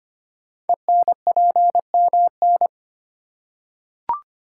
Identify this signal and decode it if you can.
Morse code, heard as a keyed tone, about 25 wpm, 700 Hz. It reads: ENPMN
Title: Code Groups - 5 Letter:Numbers - 25wpm